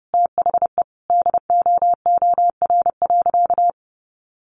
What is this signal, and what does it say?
Morse code, heard as a keyed tone, about 30 wpm, 700 Hz. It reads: THE DOOR.
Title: Sentences from Top 1200 Words 30wpm